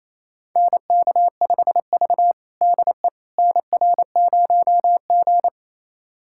Morse code, heard as a keyed tone, about 28 wpm, 700 Hz. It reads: NK5V DE NR0G
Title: USA Callsign DE Callsign - 4 Characters - 28wpm